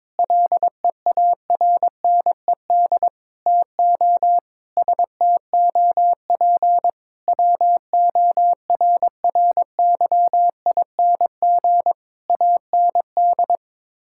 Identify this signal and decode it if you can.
Morse code, heard as a keyed tone, about 22 wpm, 700 Hz. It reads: LEARNED TO STOP WORRYING AND
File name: practice / 22wpm / Sets of 8 Words - 22wpm